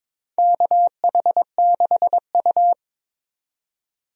Morse code, heard as a keyed tone, about 22 wpm, 700 Hz. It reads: KH6U
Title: DXCC-100 Call Signs - 4 Characters - 22wpm